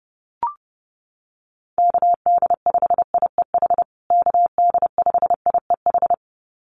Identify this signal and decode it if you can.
Morse code, heard as a keyed tone, about 30 wpm, 700 Hz. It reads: KD5IEH KD5IEH
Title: USA Call Signs Repeated 2x 30wpm